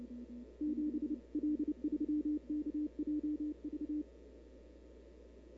Morse code, heard as a keyed tone, about 29 wpm, 300 Hz. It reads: 7 L3KJV
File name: CW / 2005-k5tr arrl cw dx -00:00